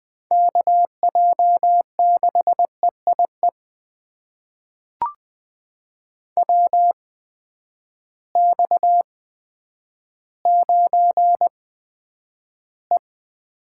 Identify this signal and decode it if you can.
Morse code, heard as a keyed tone, about 20 wpm, spaced as for 10 wpm, 700 Hz. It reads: KJ6EIE WX9E
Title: USA Call Signs - 8x Character Spacing - 20wpm